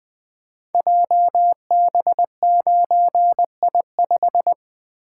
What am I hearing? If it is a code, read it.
Morse code, heard as a keyed tone, about 20 wpm, 700 Hz. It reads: JB9I5
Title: Code Groups - 5 Letter:Numbers - 20wpm